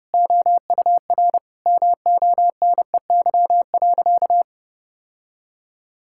Morse code, heard as a keyed tone, about 30 wpm, 700 Hz. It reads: OUR MONEY.
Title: Sentences from Top 1600 Words 30wpm